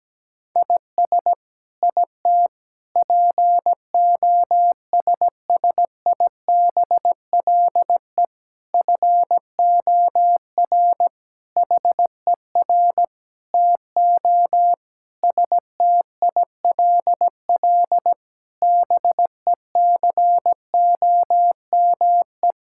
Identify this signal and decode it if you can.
Morse code, heard as a keyed tone, about 17 wpm, 700 Hz. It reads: IS IT POSSIBLE FOR HER TO STILL BECOME
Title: Sentences from Top 1000 Words - Review Part 6 - 17wpm